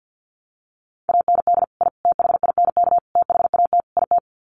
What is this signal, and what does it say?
Morse code, heard as a keyed tone, about 50 wpm, 700 Hz. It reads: AND I THINK THAT IT
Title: Sets of 5 Words Encore2 50wpm